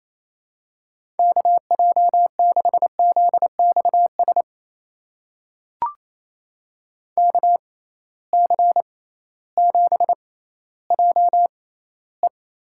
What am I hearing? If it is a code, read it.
Morse code, heard as a keyed tone, about 28 wpm, spaced as for 15 wpm, 700 Hz. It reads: KJ6ZXS KC7JE
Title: USA Call Signs - 6 Characters - 6x Character Spacing - 28wpm